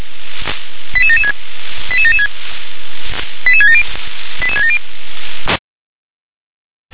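Five-tone selective call (ZVEI), heard as quick runs of five tones, tones of about 70 ms. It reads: 88776 88776 88670 88670
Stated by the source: ZVEI five-tone selective call received on an SDR.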